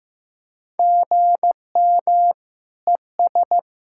Morse code, heard as a keyed tone, about 15 wpm, 700 Hz. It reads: GM ES
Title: K1USN SST QSO Practice 15wpm